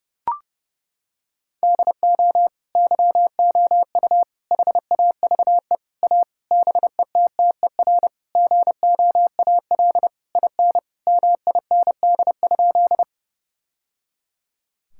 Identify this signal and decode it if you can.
Morse code, heard as a keyed tone, about 30 wpm, 700 Hz. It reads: DO YOU HAVE A BETTER GOAL IN MIND?
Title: Sentences from Top 1000 Words - Review Part 5 - 30wpm